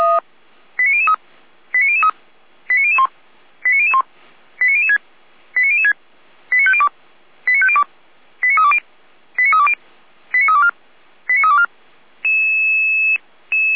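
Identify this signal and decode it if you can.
Five-tone selective call (ZVEI), heard as quick runs of five tones, tones of about 70 ms. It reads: 89003 89003 89001 89001 89006 89006 89502 89502 89320 89320 89324 89324
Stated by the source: ZVEI five-tone selective call 89160 received on an SDR, logged by Feuerwehr Orlamünde (feuerwehr-orlamuende.org).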